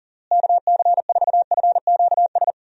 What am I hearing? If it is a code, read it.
Morse code, heard as a keyed tone, about 40 wpm, 700 Hz. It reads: KC4FQS
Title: USA Call Signs - 6 Characters - Repeated 3x - 40wpm